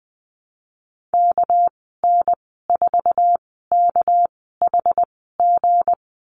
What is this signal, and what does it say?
Morse code, heard as a keyed tone, about 20 wpm, spaced as for 15 wpm, 700 Hz. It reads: KN4KHG
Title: 213 Lesson 34g - Number 8 - Call Signs 20wpm